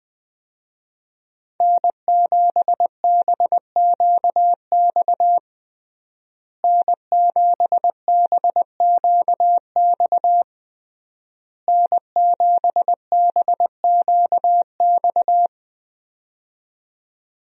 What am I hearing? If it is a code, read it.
Morse code, heard as a keyed tone, about 20 wpm, 700 Hz. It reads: N7BQX N7BQX N7BQX
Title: USA Call Signs - 5 Characters - Repeated 3x - 20wpm